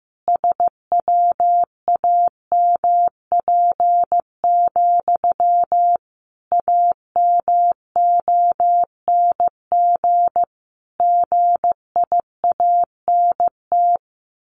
Morse code, heard as a keyed tone, about 15 wpm, 700 Hz. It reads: SWAMP, AMONG GIANT
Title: A Sound Of Thunder 15wpm